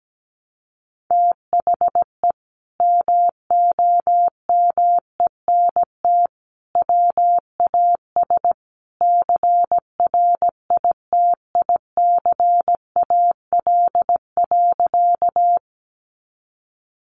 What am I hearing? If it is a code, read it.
Morse code, heard as a keyed tone, about 17 wpm, 700 Hz. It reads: THE MOMENT WAS CRITICAL.
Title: Sentences from Top 1100 Words 17wpm